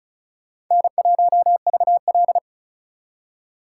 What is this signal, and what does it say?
Morse code, heard as a keyed tone, about 44 wpm, 700 Hz. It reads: N1VL